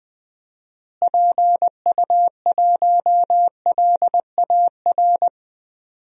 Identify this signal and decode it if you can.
Morse code, heard as a keyed tone, about 20 wpm, 700 Hz. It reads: PU1LAR
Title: DXCC-100 Call Signs - 6 Characters - Repeated 3x - 20wpm